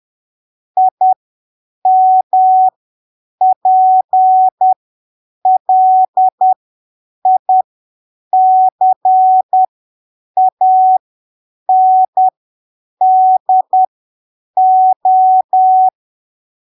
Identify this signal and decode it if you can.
Morse code, heard as a keyed tone, about 10 wpm, spaced as for 7 wpm, 750 Hz. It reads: IMPLICANDO
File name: arquivos / texto / texto 22 10ppm